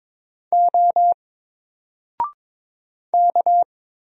Morse code, heard as a keyed tone, about 22 wpm, 700 Hz. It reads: O K
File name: Single-Letter-Number-22wpm